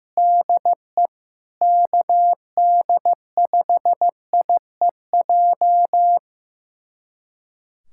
Morse code, heard as a keyed tone, about 15 wpm, 700 Hz. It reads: DE KD5IEJ
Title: USA Callsign DE Callsign - 6 Characters - 15wpm